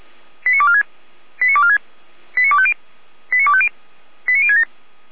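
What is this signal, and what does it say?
Five-tone selective call (ZVEI), heard as quick runs of five tones, tones of about 70 ms. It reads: 89246 89246 89250 89250 89067
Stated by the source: ZVEI five-tone selective call received with an SDR.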